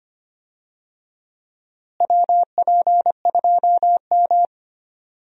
Morse code, heard as a keyed tone, about 25 wpm, 700 Hz. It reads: WP2M
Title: USA Call Signs - 4 Characters - Repeated 3x - 25wpm